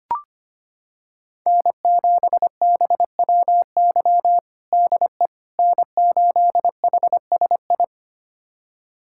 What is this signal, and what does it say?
Morse code, heard as a keyed tone, about 25 wpm, 700 Hz. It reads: N7BWY DE N8HSI
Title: USA Callsign DE Callsign - 5 Characters - 25wpm